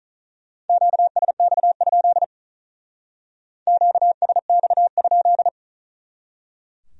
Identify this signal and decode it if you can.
Morse code, heard as a keyed tone, about 46 wpm, 700 Hz. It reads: QSX? QSX?